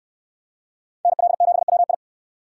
Morse code, heard as a keyed tone, about 67 wpm, 700 Hz. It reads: NL7WI